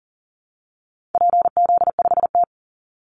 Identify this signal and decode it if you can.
Morse code, heard as a keyed tone, about 40 wpm, 700 Hz. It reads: PZ5T
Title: Call Signs Repeated 2x - 40wpm